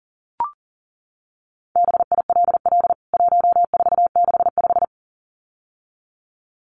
Morse code, heard as a keyed tone, about 37 wpm, 700 Hz. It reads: BILL 1465
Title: CWT Exchange 40wpm